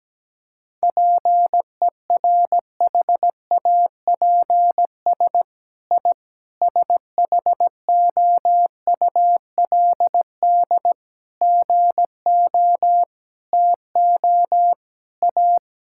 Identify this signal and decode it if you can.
Morse code, heard as a keyed tone, about 17 wpm, 700 Hz. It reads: PERHAPS I SHOULD GO TO A
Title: Sentences from Top 1000 Words - Review Part 5 - 17wpm